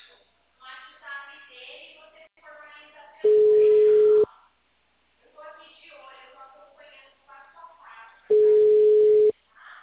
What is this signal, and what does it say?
Ringback tone (Europe, 425 Hz), with speech.